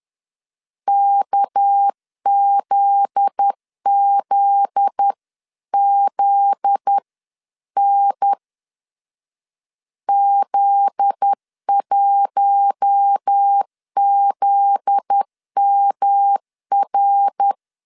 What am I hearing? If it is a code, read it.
Morse code, heard as a keyed tone, about 10 wpm, 800 Hz. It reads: KZZZN Z1ZMR
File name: LEZIONI / 7 GF9Z / 7 GF9Z 10wpm 2611